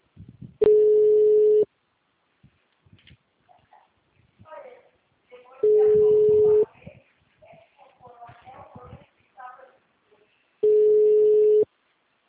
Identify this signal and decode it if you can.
Ringback tone (Europe, 425 Hz), with speech.
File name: out- -2012- - - .9204